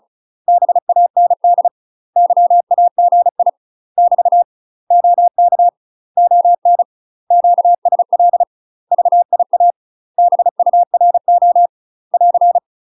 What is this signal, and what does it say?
Morse code, heard as a keyed tone, about 36 wpm, 700 Hz. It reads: BAND YAGI <BT> OK ON QSL VIA BURO <AR>
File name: UK MASTER QSOs TRAINING / 35WPM / UK MASTER QSOs 0033